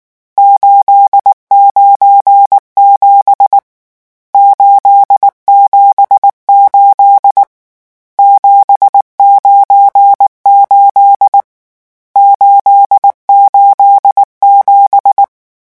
Morse code, heard as a keyed tone, about 19 wpm, 800 Hz. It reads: 897 878 798 887
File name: morse mp3 / lecon12-vitesse 18